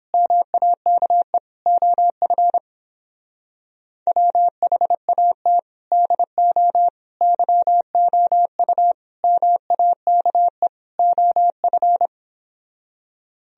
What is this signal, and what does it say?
Morse code, heard as a keyed tone, about 29 wpm, 700 Hz. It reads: MAKE OF WHAT DO YOU MAKE OF